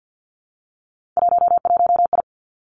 Morse code, heard as a keyed tone, about 50 wpm, 700 Hz. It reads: 11I
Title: Three Challenging Letter-Number Combinations 50wpm